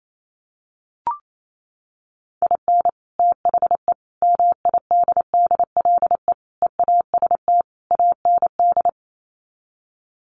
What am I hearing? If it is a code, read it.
Morse code, heard as a keyed tone, about 28 wpm, 700 Hz. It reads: IN THE MIDDLE EAST AND
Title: Sets of 5 Words Encore 28wpm